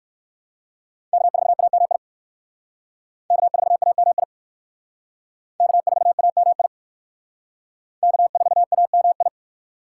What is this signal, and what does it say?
Morse code, heard as a keyed tone, about 54 wpm, 700 Hz. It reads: K4AMI K4AMI K4AMI K4AMI
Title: Speed Racing - USA 1x3 Call Signs 45wpm